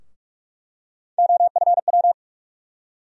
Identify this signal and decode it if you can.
Morse code, heard as a keyed tone, about 41 wpm, 700 Hz. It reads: OFW
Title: USA Call Sign Postfixes - 3 Letters 45wpm